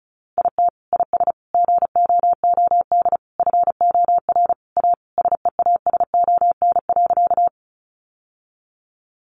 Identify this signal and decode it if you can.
Morse code, heard as a keyed tone, about 34 wpm, 700 Hz. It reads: IT IS GOOD FOR A SEASON.
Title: Sentences from Top 1400 Words 35wpm